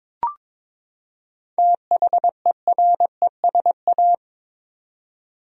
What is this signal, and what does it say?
Morse code, heard as a keyed tone, about 22 wpm, 700 Hz. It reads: THERESA
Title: USA Common Names - 22wpm